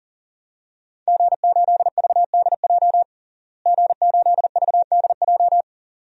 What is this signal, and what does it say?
Morse code, heard as a keyed tone, about 40 wpm, 700 Hz. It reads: G8VDJ G8VDJ
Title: DXCC-100 Call Signs - 5 Characters - Repeated 2x - 40wpm